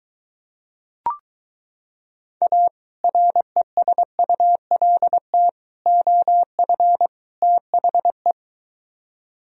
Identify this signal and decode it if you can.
Morse code, heard as a keyed tone, about 22 wpm, 700 Hz. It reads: A RESULT OF THE